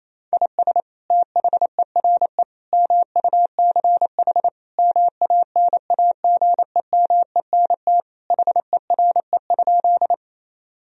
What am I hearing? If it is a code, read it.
Morse code, heard as a keyed tone, about 27 wpm, 700 Hz. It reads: IS THERE MUCH MANAGEMENT HERE?
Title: Sentences from Top 1000 Words - Review Part 4 - 28wpm